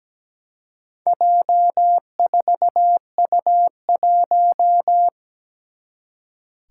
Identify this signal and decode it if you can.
Morse code, heard as a keyed tone, about 17 wpm, 700 Hz. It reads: J4U1
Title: Four Challenging Letter-Number Combinations 17wpm